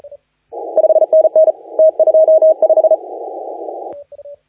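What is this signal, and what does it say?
Morse code, heard as a keyed tone, about 33 wpm, 630 Hz. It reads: 5NN T25
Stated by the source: Morse contest audio recorded by LA4C during the SAC CW 2018 contest.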